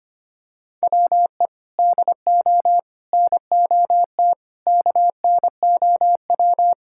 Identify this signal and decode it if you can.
Morse code, heard as a keyed tone, about 25 wpm, 700 Hz. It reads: WE DO NOT KNOW
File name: practice / 25wpm / Sentences with 4-Word Phrases - 25wpm